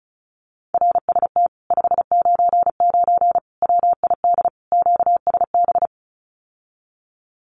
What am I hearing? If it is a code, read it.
Morse code, heard as a keyed tone, about 34 wpm, 700 Hz. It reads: RST 599 WID QSB
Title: Common QSO Phrases Encore 35wpm